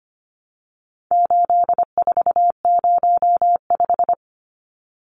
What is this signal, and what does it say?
Morse code, heard as a keyed tone, about 25 wpm, 700 Hz. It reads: 8405
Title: Numbers - 4-Digit 25wpm